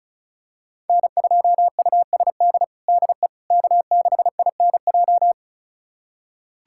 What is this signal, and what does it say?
Morse code, heard as a keyed tone, about 34 wpm, 690 Hz. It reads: N2USD DE K6INJ
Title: USA Callsign DE Callsign - 5 Characters - 35wpm